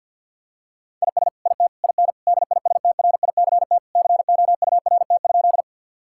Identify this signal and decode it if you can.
Morse code, heard as a keyed tone, about 49 wpm, 700 Hz. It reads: IS IT IN DISTRICT COURT?
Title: Sentences from Top 1600 Words 50wpm